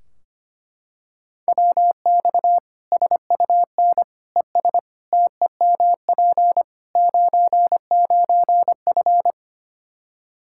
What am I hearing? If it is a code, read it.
Morse code, heard as a keyed tone, about 25 wpm, 700 Hz. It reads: WX SUN ES TEMP 99F
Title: Practice QSO - Exchange 2 - Info - 25wpm